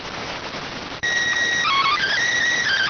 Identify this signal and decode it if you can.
SSTV (slow-scan television), PD 290.